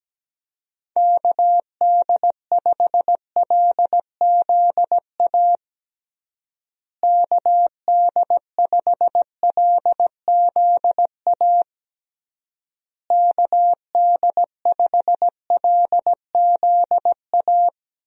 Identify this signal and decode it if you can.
Morse code, heard as a keyed tone, about 17 wpm, 700 Hz. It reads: KD5LZA KD5LZA KD5LZA
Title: USA Call Signs - 6 Characters - Repeated 3x - 17wpm